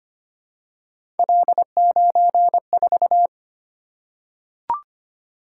Morse code, heard as a keyed tone, about 25 wpm, 700 Hz. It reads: L94
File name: Three Letter-Number Combinations - 25wpm